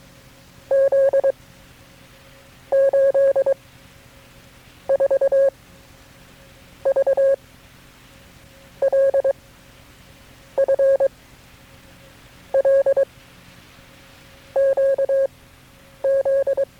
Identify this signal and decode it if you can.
Morse code, heard as a keyed tone, about 22 wpm, spaced as for 9 wpm, 540 Hz. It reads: Z 8 4 V L F L QZ
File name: KB6MT-tape3-side1